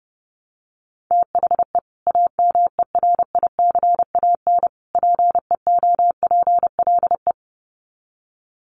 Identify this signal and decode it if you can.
Morse code, heard as a keyed tone, about 30 wpm, 700 Hz. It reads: THE AMERICAN PEOPLE
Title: Sets of 3 Words 30wpm